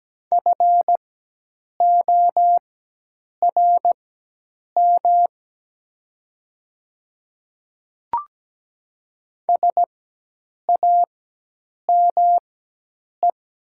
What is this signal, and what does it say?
Morse code, heard as a keyed tone, about 17 wpm, spaced as for 8 wpm, 700 Hz. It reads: FORM SAME
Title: 4x Character Spacing - Top 300 Words - 17wpm